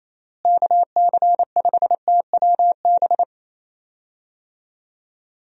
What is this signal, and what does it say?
Morse code, heard as a keyed tone, about 27 wpm, 700 Hz. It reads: KC5TWB
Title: USA Call Signs - 6 Characters - Repeated 2x - 28wpm